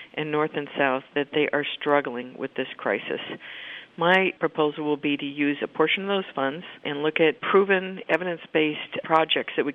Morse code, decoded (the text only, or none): none